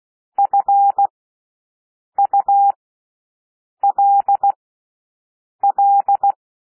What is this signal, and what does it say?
Morse code, heard as a keyed tone, about 16 wpm, spaced as for 6 wpm, 800 Hz. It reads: FULL
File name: cwmp3 / W1 SOUND WORDS 16 wpm